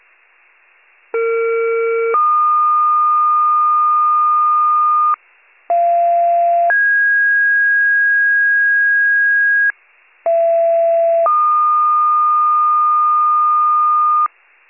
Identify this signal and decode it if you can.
Two-tone page, heard as a one-second tone, then a three-second tone, about 460 then 1220 Hz, about 690 then 1650 Hz, about 670 then 1220 Hz. It is heard from 1.1 to 14.3 s.